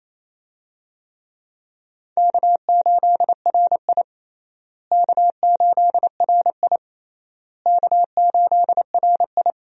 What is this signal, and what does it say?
Morse code, heard as a keyed tone, about 27 wpm, 700 Hz. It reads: K8RI K8RI K8RI
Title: USA Call Signs - 4 Characters - Repeated 3x - 28wpm